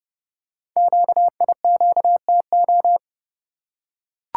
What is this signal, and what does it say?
Morse code, heard as a keyed tone, about 30 wpm, 690 Hz. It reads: QIQTO
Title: Code Groups - 5 Letters - 30wpm